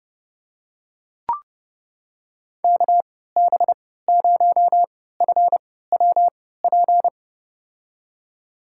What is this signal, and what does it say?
Morse code, heard as a keyed tone, about 32 wpm, spaced as for 17 wpm, 700 Hz. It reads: KB0FWP
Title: USA Call Signs - 6 Characters - 3x Character Spacing - 30wpm